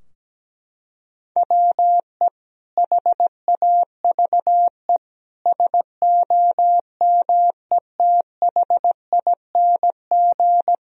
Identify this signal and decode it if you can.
Morse code, heard as a keyed tone, about 17 wpm, 700 Hz. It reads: WE HAVE SOMETHING
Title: Sentences from Top 1000 Words - Review Part 4 - 17wpm